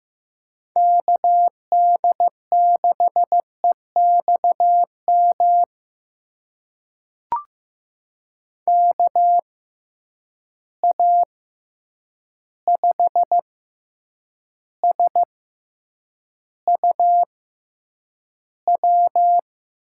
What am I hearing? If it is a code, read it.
Morse code, heard as a keyed tone, about 15 wpm, spaced as for 8 wpm, 700 Hz. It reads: KD6EXM KA5SUW
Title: USA Call Signs - 6 Characters - 6x Character Spacing - 15wpm